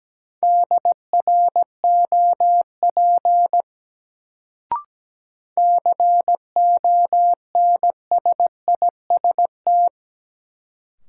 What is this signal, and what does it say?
Morse code, heard as a keyed tone, about 17 wpm, 700 Hz. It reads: DROP CONSIST
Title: Sentences from Top 1400 Words - New Word Drill Alt - 17wpm